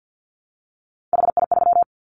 Morse code, heard as a keyed tone, about 50 wpm, 700 Hz. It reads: HI3
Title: Callsign Prefixes for Top 100 DXCC - 50wpm